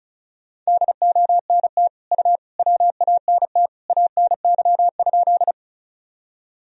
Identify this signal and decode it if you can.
Morse code, heard as a keyed tone, about 36 wpm, 700 Hz. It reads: DONT U WANT ANY?
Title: Sentences from Top 300 Words and Abbreviations 35wpm